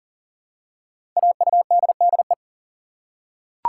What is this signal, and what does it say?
Morse code, heard as a keyed tone, about 39 wpm, 690 Hz. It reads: AUDDE
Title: Code Groups - 5 Letters - 40wpm